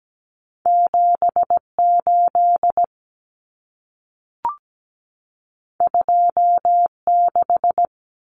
Morse code, heard as a keyed tone, about 17 wpm, 700 Hz. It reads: 78 26
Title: Common Cut Numbers - 2-Digit 17wpm